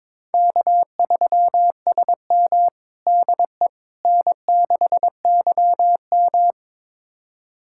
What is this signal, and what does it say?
Morse code, heard as a keyed tone, about 22 wpm, 700 Hz. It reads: K3SM DE N6YM
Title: USA Callsign DE Callsign - 4 Characters - 22wpm